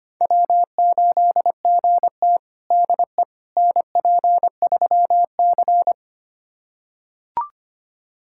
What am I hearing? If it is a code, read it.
Morse code, heard as a keyed tone, about 25 wpm, 700 Hz. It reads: W8GT DE NP3C
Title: USA Callsign DE Callsign - 4 Characters - 25wpm